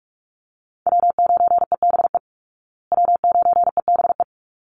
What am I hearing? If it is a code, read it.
Morse code, heard as a keyed tone, about 44 wpm, 700 Hz. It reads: W9EBE W9EBE
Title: Call Signs Repeated 3x - 45wpm